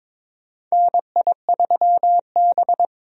Morse code, heard as a keyed tone, about 22 wpm, 700 Hz. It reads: NI3B
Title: USA Call Signs - 4 Characters - Repeated 2x - 22wpm